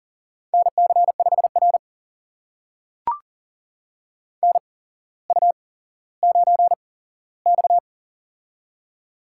Morse code, heard as a keyed tone, about 39 wpm, spaced as for 17 wpm, 700 Hz. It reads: NC5R NU9X
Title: USA Call Signs - 4 Characters - 8x Character Spacing - 40wpm